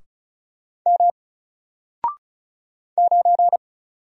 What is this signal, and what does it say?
Morse code, heard as a keyed tone, about 36 wpm, 700 Hz. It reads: M 9
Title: Single Character (Letter:Number:Punctuation:Prosign) Alt 35wpm